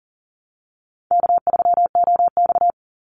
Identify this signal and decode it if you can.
Morse code, heard as a keyed tone, about 40 wpm, 700 Hz. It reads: K3OX
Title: Easy Call Signs Repeated 3x - 40wpm